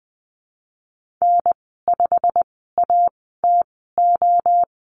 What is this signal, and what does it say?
Morse code, heard as a keyed tone, about 20 wpm, spaced as for 15 wpm, 700 Hz. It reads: N5ATO
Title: 186 Lesson 30g - Letter b - Call Signs 20wpm